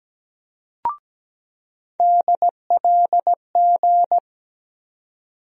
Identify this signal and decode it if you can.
Morse code, heard as a keyed tone, about 17 wpm, 700 Hz. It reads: DLG
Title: USA Call Sign Postfixes - 3 Letters 17wpm